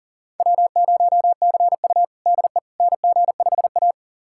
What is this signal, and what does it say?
Morse code, heard as a keyed tone, about 40 wpm, 700 Hz. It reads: W0CU DE NG5A
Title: USA Callsign DE Callsign - 4 Characters - 40wpm